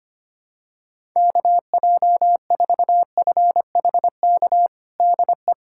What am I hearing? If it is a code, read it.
Morse code, heard as a keyed tone, about 25 wpm, 700 Hz. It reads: KJ4FHK DE
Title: USA Callsign DE Callsign 25wpm